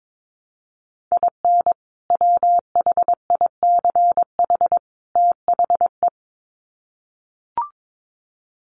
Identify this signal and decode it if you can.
Morse code, heard as a keyed tone, about 22 wpm, 700 Hz. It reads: IN WHICH THE
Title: Sets of 3 Words 22wpm